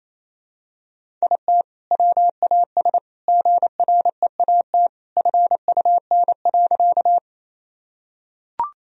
Morse code, heard as a keyed tone, about 28 wpm, 700 Hz. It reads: IT WAS GREAT FUN.
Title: Sentences from Top 1000 Words - Review Part 4 - 28wpm